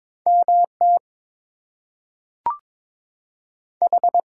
Morse code, heard as a keyed tone, about 22 wpm, 700 Hz. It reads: MT H
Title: US States and Canadian Provinces 22wpm